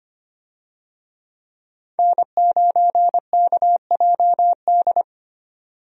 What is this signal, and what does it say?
Morse code, heard as a keyed tone, about 25 wpm, 690 Hz. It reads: N9KJD
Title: USA Call Signs - 5 Characters - 25wpm